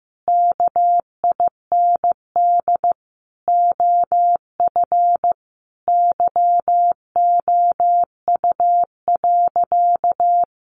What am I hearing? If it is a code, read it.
Morse code, heard as a keyed tone, about 14 wpm, 700 Hz. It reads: KIND OF YOU.
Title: Sentences from Top 200 Words 15wpm